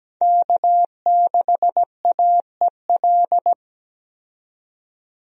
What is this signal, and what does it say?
Morse code, heard as a keyed tone, about 17 wpm, 700 Hz. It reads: K6AEL
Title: USA Call Signs - 5 Characters - 17wpm